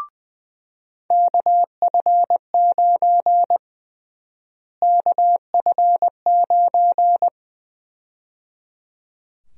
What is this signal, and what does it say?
Morse code, heard as a keyed tone, about 20 wpm, 700 Hz. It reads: KF9 KF9
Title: USA Call Sign Prefixes - Repeated 2x - 20wpm